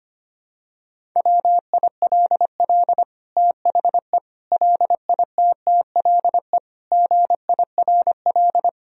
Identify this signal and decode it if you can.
Morse code, heard as a keyed tone, about 25 wpm, 700 Hz. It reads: WILL THE LITTLE GIRL
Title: Sentences from Top 1000 Words - Review Part 1 - 25wpm